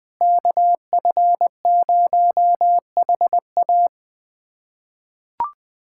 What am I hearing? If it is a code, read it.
Morse code, heard as a keyed tone, about 21 wpm, 700 Hz. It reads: KF0HA
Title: USA Callsign DE Callsign 20wpm